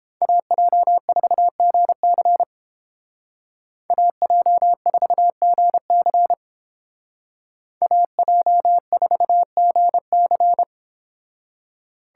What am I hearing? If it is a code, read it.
Morse code, heard as a keyed tone, about 29 wpm, 700 Hz. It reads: AJ4GC AJ4GC AJ4GC